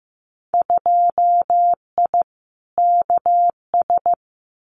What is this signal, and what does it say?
Morse code, heard as a keyed tone, about 15 wpm, 700 Hz. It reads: 2I KS
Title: Winter Field Day Exchanges Encore 15wpm